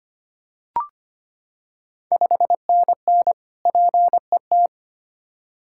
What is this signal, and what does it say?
Morse code, heard as a keyed tone, about 25 wpm, 700 Hz. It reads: 5NN PET
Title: Ontario QSO Party Exchange 25wpm